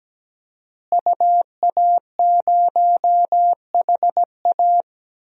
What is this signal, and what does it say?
Morse code, heard as a keyed tone, about 17 wpm, 700 Hz. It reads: UA0HA
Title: DXCC-100 Call Signs - 5 Characters - Repeated 3x - 17wpm